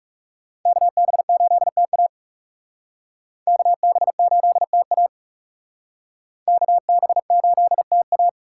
Morse code, heard as a keyed tone, about 36 wpm, 700 Hz. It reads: KB8TA KB8TA KB8TA
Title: Speed Racing - USA 2x2 Call Signs 30wpm